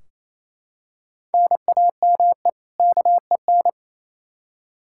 Morse code, heard as a keyed tone, about 28 wpm, 700 Hz. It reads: NAME KEN
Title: Practice QSO - Exchange 1 - Beginning - 28wpm